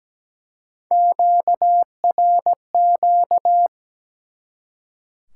Morse code, heard as a keyed tone, about 17 wpm, 700 Hz. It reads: QRQ
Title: Top 180 QSO Elements 17wpm